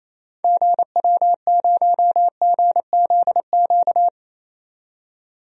Morse code, heard as a keyed tone, about 28 wpm, 700 Hz. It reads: GW0GZQ
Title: DXCC-100 Call Signs - 6 Characters - 28wpm